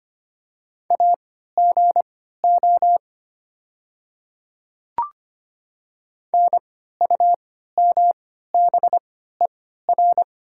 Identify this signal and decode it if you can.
Morse code, heard as a keyed tone, about 25 wpm, spaced as for 14 wpm, 700 Hz. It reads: AGO NUMBER
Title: 3x Character Spacing - Top 500 Words - 25wpm